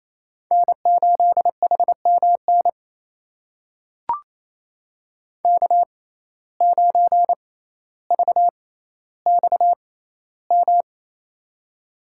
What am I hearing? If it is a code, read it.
Morse code, heard as a keyed tone, about 28 wpm, spaced as for 14 wpm, 700 Hz. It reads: N8HMN K9VXM
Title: USA Call Signs - 5 Characters - 6x Character Spacing - 28wpm